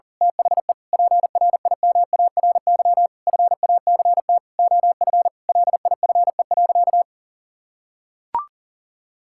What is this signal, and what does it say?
Morse code, heard as a keyed tone, about 40 wpm, 700 Hz. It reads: THE PRIMARY FACT OF LIFE.
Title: Sentences from Top 1000 Words - Review Part 5 - 40wpm